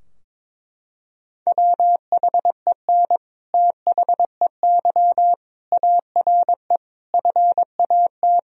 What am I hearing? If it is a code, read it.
Morse code, heard as a keyed tone, about 22 wpm, 700 Hz. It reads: WHEN THEY ARE FAT
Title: Sentences from Top 1000 Words - Review Part 4 - 22wpm